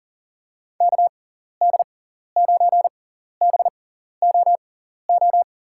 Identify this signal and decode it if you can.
Morse code, heard as a keyed tone, about 40 wpm, spaced as for 14 wpm, 700 Hz. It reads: KD9BOO
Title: USA Call Signs - 6 Characters - 6x Character Spacing - 40wpm